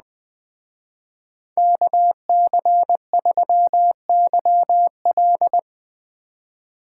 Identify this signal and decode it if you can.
Morse code, heard as a keyed tone, about 20 wpm, 700 Hz. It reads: KC3YL
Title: Speed Racing - USA Call Signs 17wpm